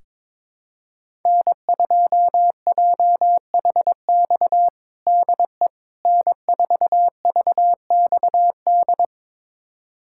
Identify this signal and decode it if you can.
Morse code, heard as a keyed tone, about 22 wpm, 700 Hz. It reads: N2JHX DE N4VXD
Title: USA Callsign DE Callsign - 5 Characters - 22wpm